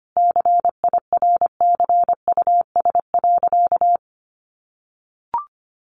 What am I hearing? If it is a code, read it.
Morse code, heard as a keyed tone, about 25 wpm, 700 Hz. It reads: CIRCUS.
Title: Easy Phrases 25wpm